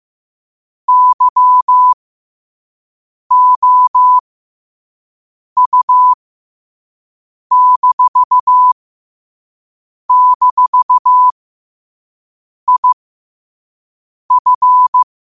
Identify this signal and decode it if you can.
Morse code, heard as a keyed tone, about 15 wpm, spaced as for 5 wpm, 1000 Hz. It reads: YOU--IF
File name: 5wpm